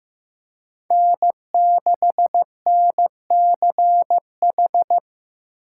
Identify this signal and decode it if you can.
Morse code, heard as a keyed tone, about 15 wpm, 700 Hz. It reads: N6NCH
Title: USA Call Signs - 5 Characters - 2x Character Spacing - 15wpm